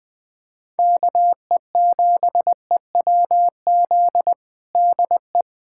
Morse code, heard as a keyed tone, about 21 wpm, 700 Hz. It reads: KE7EWZ DE
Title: USA Callsign DE Callsign 20wpm